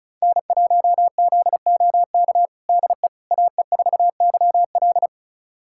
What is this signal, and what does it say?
Morse code, heard as a keyed tone, about 36 wpm, 700 Hz. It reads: N1ZOK DE AE4YL
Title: USA Callsign DE Callsign - 5 Characters - 35wpm